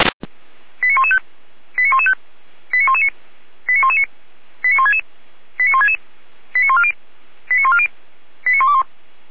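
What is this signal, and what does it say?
Five-tone selective call (ZVEI), heard as quick runs of five tones, tones of about 70 ms. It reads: 89115 89115 89119 89119 89166 89166 89140 89140 89121
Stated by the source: Five-tone ZVEI selective call 89073, received by SDR by fire brigade Orlamünde (feuerwehr-orlamuende).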